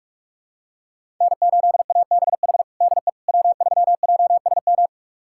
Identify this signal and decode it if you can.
Morse code, heard as a keyed tone, about 46 wpm, 700 Hz. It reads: N8ABH DE W3JSM
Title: USA Callsign DE Callsign - 5 Characters - 45wpm